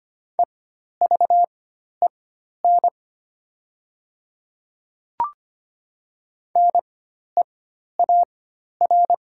Morse code, heard as a keyed tone, about 25 wpm, spaced as for 12 wpm, 700 Hz. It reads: EVEN NEAR
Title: 4x Character Spacing - Top 300 Words - 25wpm